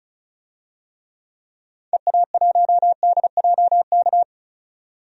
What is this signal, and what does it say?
Morse code, heard as a keyed tone, about 36 wpm, 690 Hz. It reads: EA1DJK
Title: DXCC-100 Call Signs - 6 Characters - 35wpm